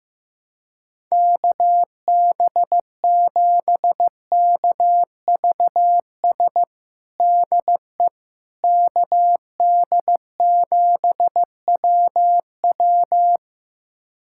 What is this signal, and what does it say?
Morse code, heard as a keyed tone, about 15 wpm, 700 Hz. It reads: KB7KVS DE KD7WW
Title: USA Callsign DE Callsign 15wpm